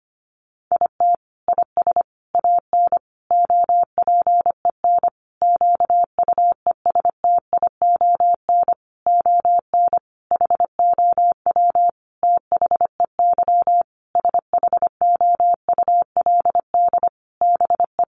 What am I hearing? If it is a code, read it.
Morse code, heard as a keyed tone, about 25 wpm, 700 Hz. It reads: IT IS AN OPEN QUESTION ON HOW THEY SHOULD BE
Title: Sentences from Top 1300 Words 25wpm